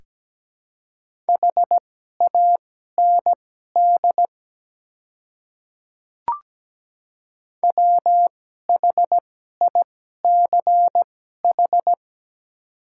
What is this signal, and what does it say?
Morse code, heard as a keyed tone, about 17 wpm, spaced as for 12 wpm, 700 Hz. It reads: HAND WHICH
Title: 2x Character Spacing - Top 200 Words - 17wpm